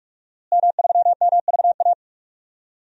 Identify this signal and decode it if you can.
Morse code, heard as a keyed tone, about 46 wpm, 700 Hz. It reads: M3MVA
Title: DXCC-100 Call Signs - 5 Characters - Repeated 2x - 45wpm